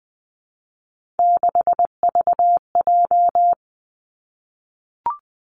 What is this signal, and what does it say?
Morse code, heard as a keyed tone, about 20 wpm, 700 Hz. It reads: 6VJ
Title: Three Challenging Letter-Number Combinations 20wpm